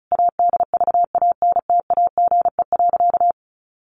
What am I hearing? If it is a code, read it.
Morse code, heard as a keyed tone, about 34 wpm, 700 Hz. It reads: ADVANTAGE.
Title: Sentences from Top 900 Words 35wpm